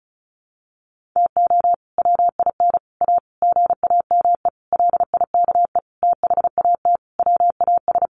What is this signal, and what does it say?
Morse code, heard as a keyed tone, about 34 wpm, 700 Hz. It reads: TO WIN A GAME LIKE THAT WAS
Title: Sentences from Top 1500 Words 35wpm